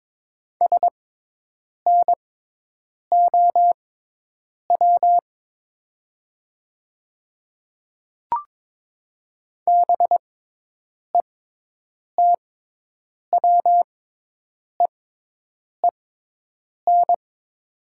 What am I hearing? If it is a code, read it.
Morse code, heard as a keyed tone, about 22 wpm, spaced as for 8 wpm, 700 Hz. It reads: SNOW BETWEEN
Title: 6x Character Spacing - Top 500 Words - 22wpm